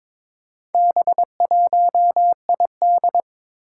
Morse code, heard as a keyed tone, about 22 wpm, 700 Hz. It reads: B1ID
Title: Four Challenging Letter-Number Combinations 22wpm